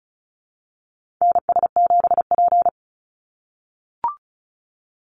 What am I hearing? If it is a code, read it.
Morse code, heard as a keyed tone, about 34 wpm, 700 Hz. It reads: NS7P
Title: NA SOTA Activator Callsigns 35wpm